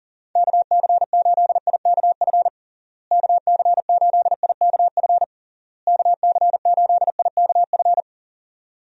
Morse code, heard as a keyed tone, about 40 wpm, 700 Hz. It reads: KC8IKF KC8IKF KC8IKF
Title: USA Call Signs - 6 Characters - Repeated 3x - 40wpm